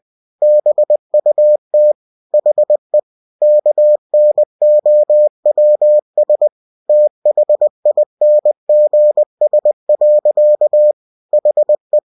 Morse code, heard as a keyed tone, about 20 wpm, 600 Hz. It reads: BUT HE KNOWS THINGS. HE
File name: cw-books / winnie-the-pooh / winnie-20wpm-page-0091